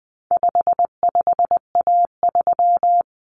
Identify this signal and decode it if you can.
Morse code, heard as a keyed tone, about 20 wpm, 700 Hz. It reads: 55A3
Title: Common Cut Numbers - 4-Digit 20wpm